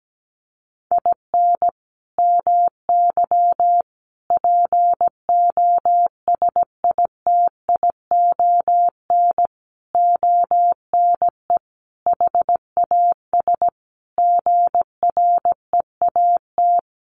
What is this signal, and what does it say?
Morse code, heard as a keyed tone, about 17 wpm, 700 Hz. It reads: IN MY POSITION ONE HAS GREAT
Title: Sentences from Top 1100 Words 17wpm